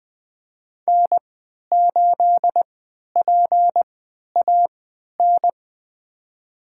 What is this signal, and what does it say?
Morse code, heard as a keyed tone, about 21 wpm, spaced as for 11 wpm, 700 Hz. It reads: N8PAN
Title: USA Call Signs - 3x Character Spacing - 20wpm